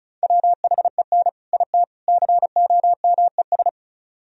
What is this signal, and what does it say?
Morse code, heard as a keyed tone, about 46 wpm, 700 Hz. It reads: WHEN IT COMES